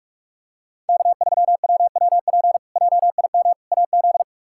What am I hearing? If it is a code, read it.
Morse code, heard as a keyed tone, about 46 wpm, 700 Hz. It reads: K3WWP JIM AZ
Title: K1USN SST QSO Practice - 45wpm